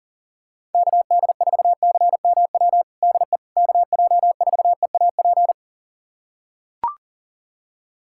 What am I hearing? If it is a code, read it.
Morse code, heard as a keyed tone, about 40 wpm, 700 Hz. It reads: KD4CMW DE KJ4EAP
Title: USA Callsign DE Callsign 40wpm